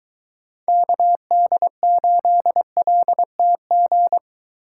Morse code, heard as a keyed tone, about 23 wpm, 700 Hz. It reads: KD8LTG